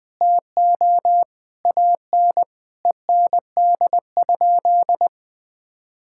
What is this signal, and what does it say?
Morse code, heard as a keyed tone, about 21 wpm, 700 Hz. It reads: TO AN END?
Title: Sentences from Top 1000 Words - Review Part 5 - 20wpm